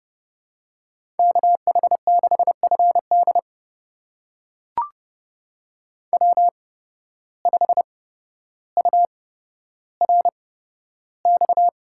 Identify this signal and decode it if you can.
Morse code, heard as a keyed tone, about 29 wpm, spaced as for 13 wpm, 700 Hz. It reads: KH6FD W5URX
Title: USA Call Signs - 5 Characters - 8x Character Spacing - 30wpm